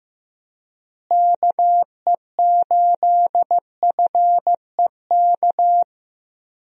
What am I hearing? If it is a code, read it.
Morse code, heard as a keyed tone, about 15 wpm, 700 Hz. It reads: KE8FEK
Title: USA Call Signs - 6 Characters - Repeated 3x - 15wpm